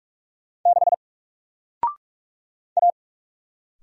Morse code, heard as a keyed tone, about 48 wpm, 700 Hz. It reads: 6 A
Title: Single Letter-Number Alt 45wpm